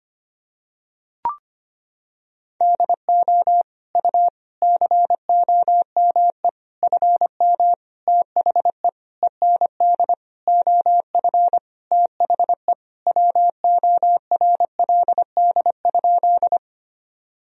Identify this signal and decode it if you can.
Morse code, heard as a keyed tone, about 25 wpm, 700 Hz. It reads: DO U COME FM THE END OF THE WORLD?
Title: Sentences from Top 200 Words and Abbreviations 25wpm